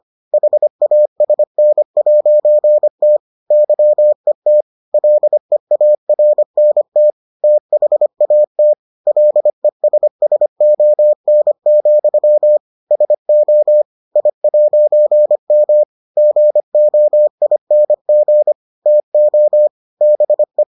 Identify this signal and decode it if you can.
Morse code, heard as a keyed tone, about 25 wpm, 600 Hz. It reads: HASN'T YET LEARNT THAT LESSON, SO I'M GOING TO BE